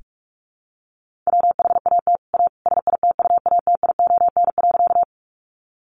Morse code, heard as a keyed tone, about 44 wpm, 700 Hz. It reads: WHAT A SITUATION.
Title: Sentences from Top 1200 Words 45wpm